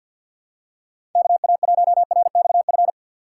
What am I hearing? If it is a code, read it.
Morse code, heard as a keyed tone, about 52 wpm, 700 Hz. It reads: KA1RXF
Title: USA Call Signs - 4x Character Spacing - 50wpm